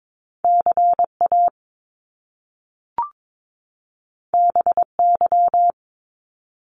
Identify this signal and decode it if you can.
Morse code, heard as a keyed tone, about 22 wpm, 700 Hz. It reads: CA BY
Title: Top Two Letter Combinations in Words 22wpm